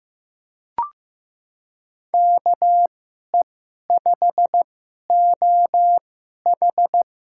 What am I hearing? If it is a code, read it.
Morse code, heard as a keyed tone, about 15 wpm, spaced as for 11 wpm, 700 Hz. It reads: KE5OH
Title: USA Call Signs - 2x Character Spacing - 15wpm